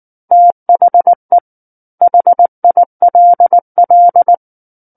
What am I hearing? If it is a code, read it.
Morse code, heard as a keyed tone, about 19 wpm, 700 Hz. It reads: THE HILL
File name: Chapter 10 - 19 WPM